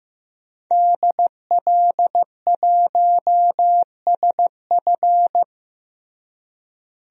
Morse code, heard as a keyed tone, about 16 wpm, 700 Hz. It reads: DL1SF
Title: DXCC-100 Call Signs - 5 Characters - Repeated 2x - 15wpm